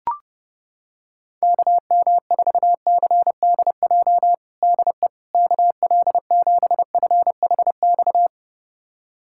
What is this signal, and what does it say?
Morse code, heard as a keyed tone, about 28 wpm, 700 Hz. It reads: KM4CDJ DE KL7FHX
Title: USA Callsign DE Callsign - 6 Characters - 30wpm